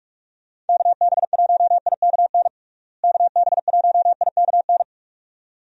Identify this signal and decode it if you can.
Morse code, heard as a keyed tone, about 47 wpm, 700 Hz. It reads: KB1IKN KB1IKN
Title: USA Call Signs - 6 Characters - Repeated 2x - 45wpm